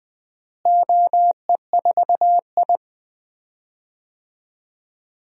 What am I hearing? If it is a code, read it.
Morse code, heard as a keyed tone, about 20 wpm, 700 Hz. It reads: OE4I
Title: DXCC-100 Call Signs - 4 Characters - Repeated 3x - 20wpm